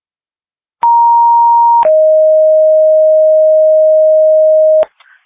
Two-tone page, heard as a one-second tone, then a three-second tone, about 940 then 620 Hz. It is heard from 0.8 to 4.8 s.
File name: dispatch / Tripoli Fire 2026 09 06 00 49 50